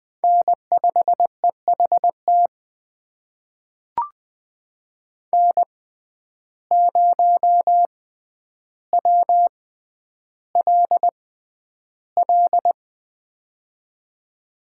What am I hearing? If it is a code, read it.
Morse code, heard as a keyed tone, about 20 wpm, spaced as for 10 wpm, 700 Hz. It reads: N5EHT N0WLL
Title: USA Call Signs - 5 Characters - 6x Character Spacing - 20wpm